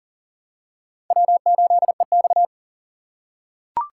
Morse code, heard as a keyed tone, about 40 wpm, 700 Hz. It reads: W8EX
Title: USA Call Signs - 4 Characters - 2x Character Spacing - 40wpm